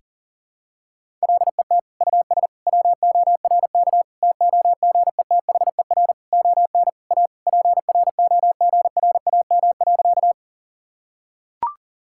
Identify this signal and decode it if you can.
Morse code, heard as a keyed tone, about 37 wpm, 700 Hz. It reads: LET US WORK TOGETHER ON A PROGRAM.
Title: Sentences from Top 1000 Words - Review Part 5 - 40wpm